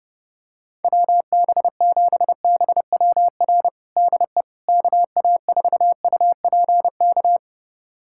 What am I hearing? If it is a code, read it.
Morse code, heard as a keyed tone, about 30 wpm, 700 Hz. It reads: WB7BWR DE KA4UPK
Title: USA Callsign DE Callsign 30wpm